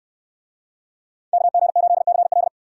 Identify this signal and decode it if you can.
Morse code, heard as a keyed tone, about 64 wpm, 700 Hz. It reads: KG8CL